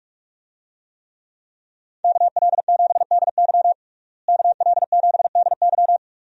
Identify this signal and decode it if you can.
Morse code, heard as a keyed tone, about 43 wpm, 700 Hz. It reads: KL7DY KL7DY
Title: DXCC-100 Call Signs - 5 Characters - Repeated 2x - 45wpm